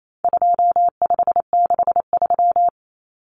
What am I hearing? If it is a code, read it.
Morse code, heard as a keyed tone, about 28 wpm, 700 Hz. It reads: 2563
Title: Common Cut Numbers - 4-Digit 28wpm